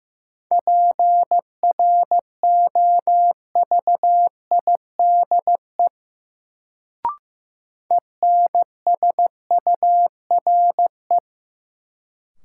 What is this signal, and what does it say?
Morse code, heard as a keyed tone, about 15 wpm, 700 Hz. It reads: PROVIDE ENSURE
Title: Sentences from Top 1100 Words - New Word Drill Alt - 15wpm